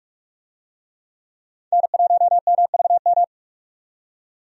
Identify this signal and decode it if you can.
Morse code, heard as a keyed tone, about 46 wpm, 700 Hz. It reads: N1MVM
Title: USA Call Signs - 5 Characters - 45wpm